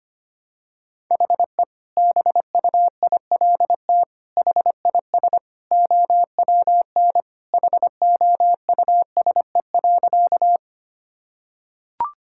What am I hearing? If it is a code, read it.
Morse code, heard as a keyed tone, about 25 wpm, 700 Hz. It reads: HE BUILT HIS OWN HOUSE.
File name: practice / 25wpm / Sentences with 2-Word Phrases - 25wpm